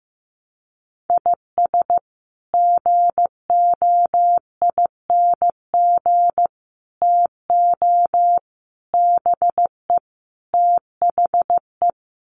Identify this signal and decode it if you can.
Morse code, heard as a keyed tone, about 15 wpm, 700 Hz. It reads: IS GOING TO BE THE
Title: Sets of 5 Words 15wpm